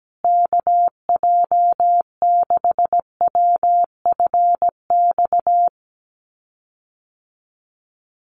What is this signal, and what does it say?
Morse code, heard as a keyed tone, about 17 wpm, 700 Hz. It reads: KJ6WFX
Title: USA Call Signs Repeated 3x 17wpm